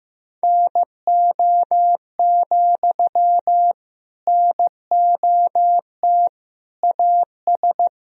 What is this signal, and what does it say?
Morse code, heard as a keyed tone, about 15 wpm, 700 Hz. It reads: NO, NOT AS
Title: Sentences from Top 1000 Words - Review Part 4 - 15wpm